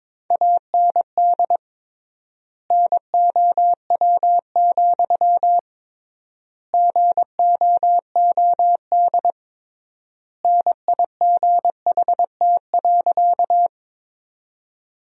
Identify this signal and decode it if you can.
Morse code, heard as a keyed tone, about 21 wpm, 700 Hz. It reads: AND NOW, GOOD NIGHT.
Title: 3x Word-Spacing - Sentences from Top 300 Words - 22wpm